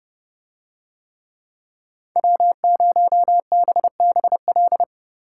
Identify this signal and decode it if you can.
Morse code, heard as a keyed tone, about 27 wpm, 700 Hz. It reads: W0BBL